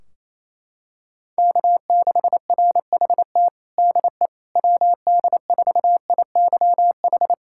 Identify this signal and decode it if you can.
Morse code, heard as a keyed tone, about 28 wpm, 700 Hz. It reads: K6RHT DE WD4IYH
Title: Practice QSO - Exchange 4 - Ending - 28wpm